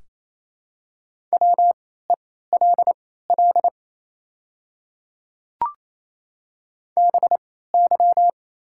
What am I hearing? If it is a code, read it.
Morse code, heard as a keyed tone, about 27 wpm, spaced as for 16 wpm, 690 Hz. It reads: WELL BY
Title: 3x Character Spacing - Top 200 Words - 28wpm